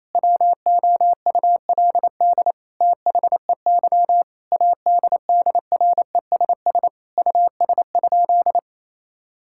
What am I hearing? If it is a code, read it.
Morse code, heard as a keyed tone, about 28 wpm, 700 Hz. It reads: WOULD THEY ADDRESS US?
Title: Sentences from Top 1000 Words - Review Part 6 - 28wpm